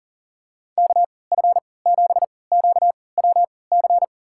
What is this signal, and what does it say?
Morse code, heard as a keyed tone, about 38 wpm, spaced as for 22 wpm, 700 Hz. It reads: KF7QWC
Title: USA Call Signs - 6 Characters - 3x Character Spacing - 40wpm